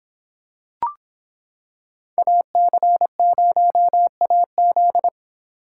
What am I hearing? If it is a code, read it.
Morse code, heard as a keyed tone, about 26 wpm, 700 Hz. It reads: AC0AZ